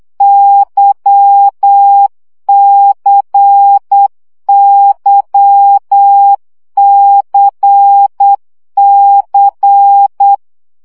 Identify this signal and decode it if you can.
Morse code, heard as a keyed tone, about 8 wpm, 800 Hz. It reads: YCYCC
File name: CW gyakorlas k c y 7 baud